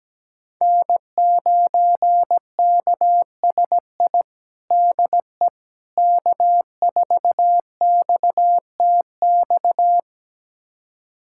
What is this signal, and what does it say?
Morse code, heard as a keyed tone, about 17 wpm, 700 Hz. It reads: N9KSI DE K4XTX
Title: USA Callsign DE Callsign - 5 Characters - 17wpm